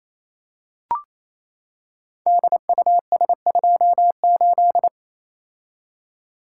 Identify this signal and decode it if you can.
Morse code, heard as a keyed tone, about 27 wpm, 700 Hz. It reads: DUS28
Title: Code Groups - 5 Letter:Numbers - 28wpm